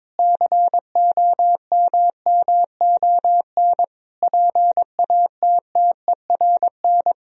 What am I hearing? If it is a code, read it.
Morse code, heard as a keyed tone, about 22 wpm, 700 Hz. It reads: COMMON PATTERN
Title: Sentences from Top 1000 Words - Review Part 2 - 22wpm